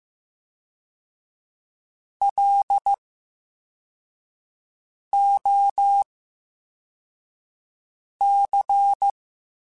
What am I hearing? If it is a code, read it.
Morse code, heard as a keyed tone, about 15 wpm, 800 Hz. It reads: L O C
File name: Special INGs-mp3s / Special INGs-mp3s / VerySlowToFaster Ing @13